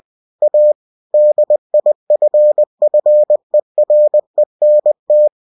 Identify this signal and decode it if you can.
Morse code, heard as a keyed tone, about 20 wpm, 600 Hz. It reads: A DIFFERENT